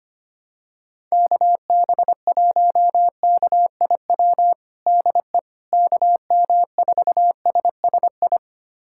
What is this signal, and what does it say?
Morse code, heard as a keyed tone, about 25 wpm, 700 Hz. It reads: KB1KIW DE KM4SSI
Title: USA Callsign DE Callsign 25wpm